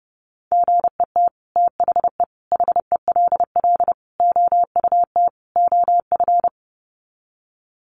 Morse code, heard as a keyed tone, about 30 wpm, 700 Hz. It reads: GET THE HELL OUT OF
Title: Sets of 5 Words Encore2 30wpm